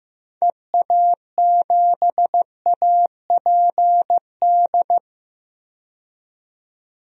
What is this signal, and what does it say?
Morse code, heard as a keyed tone, about 15 wpm, 700 Hz. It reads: EA7APD
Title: DXCC-100 Call Signs - 6 Characters - 15wpm